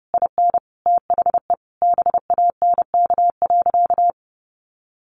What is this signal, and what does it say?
Morse code, heard as a keyed tone, about 30 wpm, 700 Hz. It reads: IN THE BANK.
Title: Sentences from Top 1200 Words 30wpm